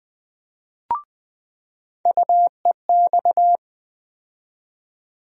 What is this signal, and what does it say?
Morse code, heard as a keyed tone, about 20 wpm, 700 Hz. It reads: UEX
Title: USA Call Sign Postfixes - 3 Letters 20wpm